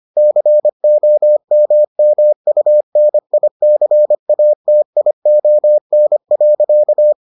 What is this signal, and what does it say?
Morse code, heard as a keyed tone, about 25 wpm, 600 Hz. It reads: COMMUNICATION.